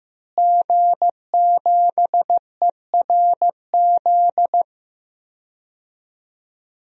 Morse code, heard as a keyed tone, about 15 wpm, 700 Hz. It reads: G7ERZ
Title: DXCC-100 Call Signs - 5 Characters - Repeated 3x - 15wpm